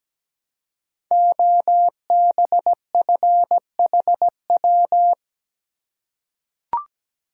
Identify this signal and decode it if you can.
Morse code, heard as a keyed tone, about 17 wpm, 700 Hz. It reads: OBFHW
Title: Code Groups - 5 Letters - 17wpm